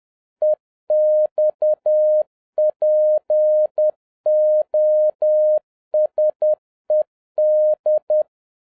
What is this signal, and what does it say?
Morse code, heard as a keyed tone, about 10 wpm, 610 Hz. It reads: EXPOSED
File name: 2011-07-mp3 / n31 THE 3.11 2 10wpm